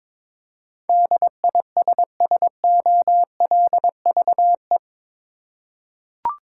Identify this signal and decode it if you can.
Morse code, heard as a keyed tone, about 22 wpm, 700 Hz. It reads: DISSOLVE
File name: practice / 22wpm / Eight Letter Words - 22wpm